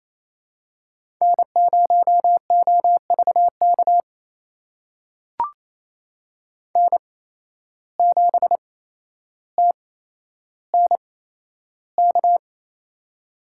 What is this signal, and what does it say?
Morse code, heard as a keyed tone, about 28 wpm, spaced as for 12 wpm, 700 Hz. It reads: N0OVK N7TNK
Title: USA Call Signs - 5 Characters - 8x Character Spacing - 28wpm